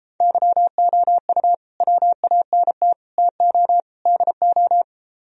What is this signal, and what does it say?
Morse code, heard as a keyed tone, about 34 wpm, 700 Hz. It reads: YOU WANT TO DO
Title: Speed-Racing - Sets of 4 Words 28wpm